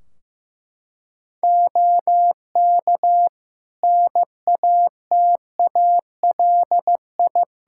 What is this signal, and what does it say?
Morse code, heard as a keyed tone, about 15 wpm, 700 Hz. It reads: OK NATALI
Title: Practice QSO - Exchange 2 - Info - 15wpm